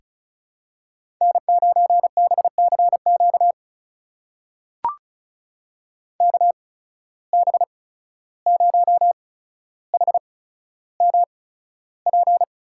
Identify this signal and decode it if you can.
Morse code, heard as a keyed tone, about 36 wpm, spaced as for 14 wpm, 700 Hz. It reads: N9BCQ KB0HMP
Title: USA Call Signs - 8x Character Spacing - 35wpm